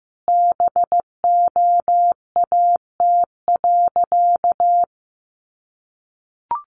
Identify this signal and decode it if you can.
Morse code, heard as a keyed tone, about 15 wpm, 700 Hz. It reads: BOAT.
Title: Sentences from Top 700 Words 15wpm